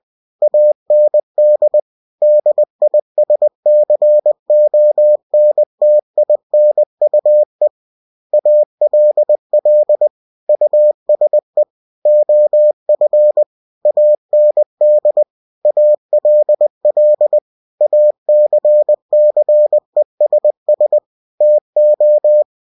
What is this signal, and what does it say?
Morse code, heard as a keyed tone, about 20 wpm, 600 Hz. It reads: AND DISCONTINUE ALL USE OF AND ALL ACCESS TO
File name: cw-books / winnie-the-pooh / winnie-20wpm-page-0117